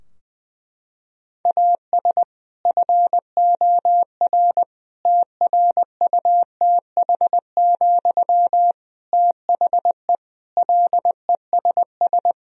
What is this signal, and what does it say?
Morse code, heard as a keyed tone, about 21 wpm, 700 Hz. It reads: AS FOR TRUTH, THE LESS
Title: Sentences from Top 1000 Words - Review Part 2 - 20wpm